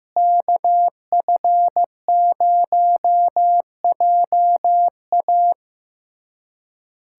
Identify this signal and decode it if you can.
Morse code, heard as a keyed tone, about 15 wpm, 700 Hz. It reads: KF0JA
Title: USA Callsign DE Callsign - 5 Characters - 15wpm